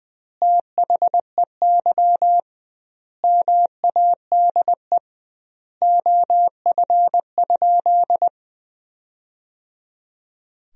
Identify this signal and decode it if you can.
Morse code, heard as a keyed tone, about 20 wpm, 700 Hz. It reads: THEY MADE OF?
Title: 2x Word-Spacing - Sentences from Top 200 Words - 20wpm